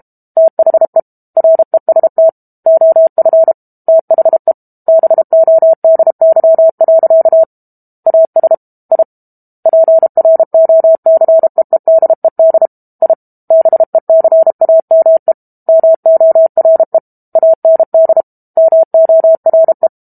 Morse code, heard as a keyed tone, about 32 wpm, 650 Hz. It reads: THE REST OF THE BODY. AS I PROCEEDED I BECAME MORE AND MORE
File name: Book Two / 06-Chapter 6 - 32 WPM